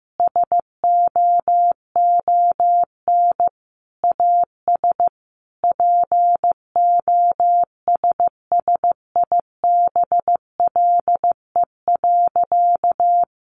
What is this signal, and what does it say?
Morse code, heard as a keyed tone, about 15 wpm, 700 Hz. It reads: SOON AS POSSIBLE.
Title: Sentences from Top 500 Words 15wpm